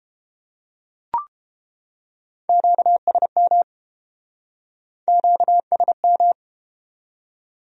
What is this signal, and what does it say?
Morse code, heard as a keyed tone, about 32 wpm, 700 Hz. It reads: QSM QSM